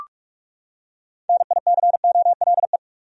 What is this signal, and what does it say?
Morse code, heard as a keyed tone, about 41 wpm, 700 Hz. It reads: NICOLE
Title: USA Common Names - 45wpm